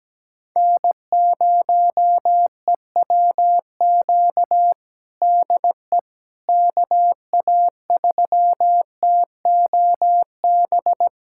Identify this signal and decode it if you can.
Morse code, heard as a keyed tone, about 17 wpm, 700 Hz. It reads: N0EWQ DE KA3TOB
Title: USA Callsign DE Callsign 17wpm